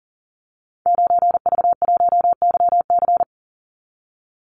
Morse code, heard as a keyed tone, about 40 wpm, 700 Hz. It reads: 9V1YC
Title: Call Signs - 40wpm